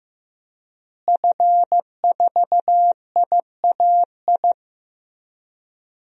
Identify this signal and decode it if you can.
Morse code, heard as a keyed tone, about 15 wpm, 700 Hz. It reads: F4IAI
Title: DXCC-100 Call Signs - 5 Characters - Repeated 3x - 15wpm